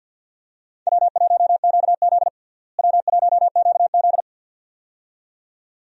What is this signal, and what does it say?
Morse code, heard as a keyed tone, about 52 wpm, 700 Hz. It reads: W1QZ W1QZ
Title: USA Call Signs - 4 Characters - Repeated 2x - 50wpm